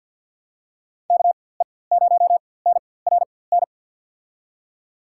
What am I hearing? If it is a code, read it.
Morse code, heard as a keyed tone, about 47 wpm, spaced as for 23 wpm, 700 Hz. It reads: KE0NRN
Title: USA Call Signs - 6 Characters - 4x Character Spacing - 50wpm